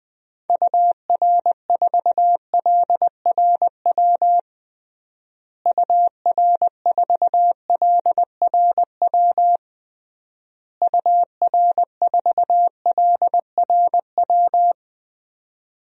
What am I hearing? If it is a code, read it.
Morse code, heard as a keyed tone, about 20 wpm, 700 Hz. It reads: UR4LRW UR4LRW UR4LRW
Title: DXCC-100 Call Signs - 6 Characters - Repeated 3x - 20wpm